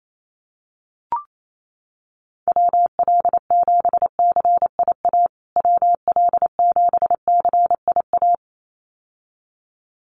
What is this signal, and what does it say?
Morse code, heard as a keyed tone, about 28 wpm, 700 Hz. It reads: WL7CIA WL7CIA
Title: USA Call Signs Repeated 2x 28wpm